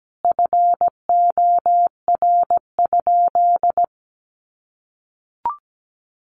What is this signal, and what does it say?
Morse code, heard as a keyed tone, about 17 wpm, 700 Hz. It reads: FOR?
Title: Sentences from Top 100 Words 17wpm